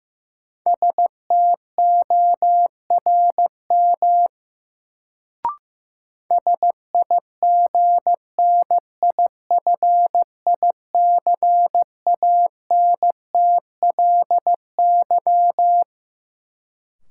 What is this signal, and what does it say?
Morse code, heard as a keyed tone, about 15 wpm, 700 Hz. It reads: STORM SIGNIFICANTLY
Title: Sentences from Top 1500 Words - New Word Drill Alt - 15wpm